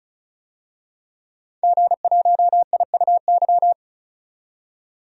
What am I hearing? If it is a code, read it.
Morse code, heard as a keyed tone, about 36 wpm, 700 Hz. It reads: G1IUY
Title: DXCC-100 Call Signs - 5 Characters - 35wpm